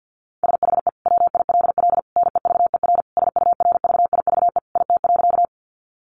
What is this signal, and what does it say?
Morse code, heard as a keyed tone, about 50 wpm, 700 Hz. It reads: SHE WILL NEVER SURVIVE IT.
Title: Sentences from Top 1300 Words 50wpm